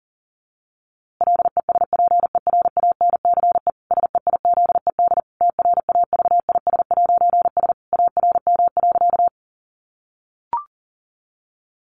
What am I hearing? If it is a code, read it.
Morse code, heard as a keyed tone, about 40 wpm, 700 Hz. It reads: LESPERANCE SEIZED TRAVIS'S ARM.
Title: A Sound Of Thunder 40wpm